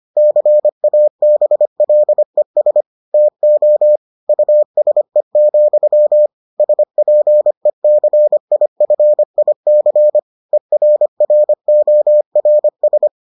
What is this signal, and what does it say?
Morse code, heard as a keyed tone, about 25 wpm, 600 Hz. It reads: CABLES TO USE, SPECIFIC ERRORS